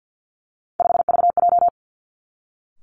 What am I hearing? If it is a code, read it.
Morse code, heard as a keyed tone, about 50 wpm, 700 Hz. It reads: 5VJ
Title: Three Challenging Letter-Number Combinations 50wpm